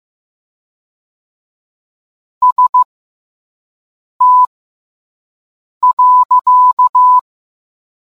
Morse code, heard as a keyed tone, about 15 wpm, spaced as for 5 wpm, 1000 Hz. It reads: ST.